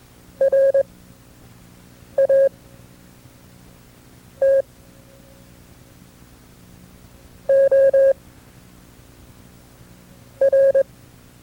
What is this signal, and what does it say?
Morse code, heard as a keyed tone, about 22 wpm, spaced as for 4 wpm, 540 Hz. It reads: RATOR
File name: FreeFiles / CW-practice / KB6MT-tape2-side2